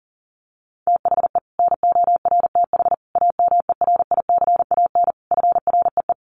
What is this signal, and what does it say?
Morse code, heard as a keyed tone, about 40 wpm, 700 Hz. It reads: THE NORTH AMERICAN FREE
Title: Sets of 5 Words Encore 40wpm